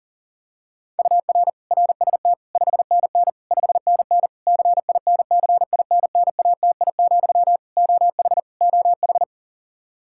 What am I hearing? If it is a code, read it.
Morse code, heard as a keyed tone, about 37 wpm, 700 Hz. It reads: UR RST 5NN 5NN CINCINNATI, OH OH
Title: Practice QSO - Exchange 1 - Beginning - 40wpm